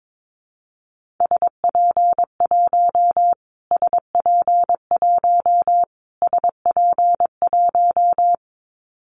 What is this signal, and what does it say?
Morse code, heard as a keyed tone, about 22 wpm, 700 Hz. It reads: SP1 SP1 SP1
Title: Callsign Prefixes for Top 25 DXCC - Repeated 3x - 22wpm